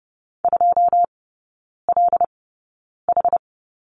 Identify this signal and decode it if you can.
Morse code, heard as a keyed tone, about 30 wpm, spaced as for 9 wpm, 700 Hz. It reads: 2LH
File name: 081 Lesson 14e - Number 2 - Three Characters 30wpm